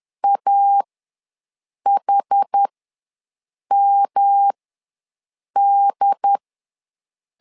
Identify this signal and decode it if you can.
Morse code, heard as a keyed tone, about 10 wpm, spaced as for 6 wpm, 800 Hz. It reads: AHMD
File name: LEZIONI / 3 MWRO / 3 M ws 10wpm 2121